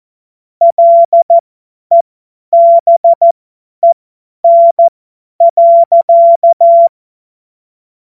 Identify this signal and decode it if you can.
Morse code, heard as a keyed tone, about 14 wpm, spaced as for 10 wpm, 680 Hz. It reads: LEBEN.
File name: Morsekurs / Witze / Witz 20. Dezember 2022 50 F70